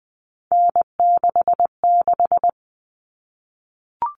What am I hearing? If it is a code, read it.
Morse code, heard as a keyed tone, about 20 wpm, 700 Hz. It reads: N66
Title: All Cut Numbers - 3-Digit 20wpm